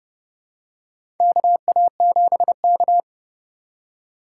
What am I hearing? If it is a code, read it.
Morse code, heard as a keyed tone, about 32 wpm, 700 Hz. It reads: KA7K
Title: USA Call Signs - 3x Character Spacing - 30wpm